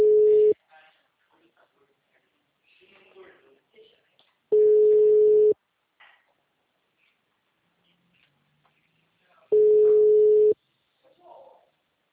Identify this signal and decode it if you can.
Ringback tone (Europe, 425 Hz), with speech.